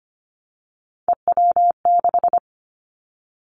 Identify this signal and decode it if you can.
Morse code, heard as a keyed tone, about 25 wpm, 700 Hz. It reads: EW6
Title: Callsign Prefixes for Top 50 DXCC - 25wpm